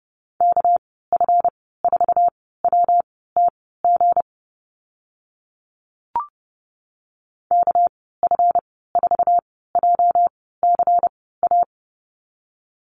Morse code, heard as a keyed tone, about 30 wpm, spaced as for 17 wpm, 700 Hz. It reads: KF4WTG KF4JCA
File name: Lesson 37g - Letter x - Call Signs 30wpm